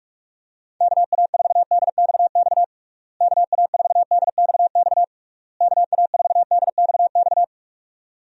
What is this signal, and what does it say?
Morse code, heard as a keyed tone, about 46 wpm, 700 Hz. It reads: KA4DXX KA4DXX KA4DXX
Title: USA Call Signs - 6 Characters - Repeated 3x - 45wpm